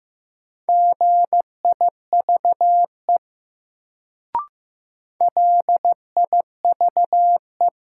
Morse code, heard as a keyed tone, about 15 wpm, 700 Hz. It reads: GIVE LIVE
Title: Sentences from Top 200 Words - New Word Drill - 15wpm